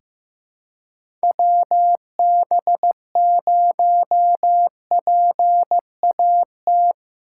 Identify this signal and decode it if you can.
Morse code, heard as a keyed tone, about 16 wpm, 700 Hz. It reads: WB0PAT
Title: USA Call Signs - 6 Characters - 3x Character Spacing - 15wpm